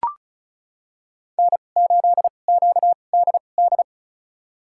Morse code, heard as a keyed tone, about 36 wpm, spaced as for 25 wpm, 700 Hz. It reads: N8QDD
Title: USA Call Signs - 5 Characters - 2x Character Spacing - 35wpm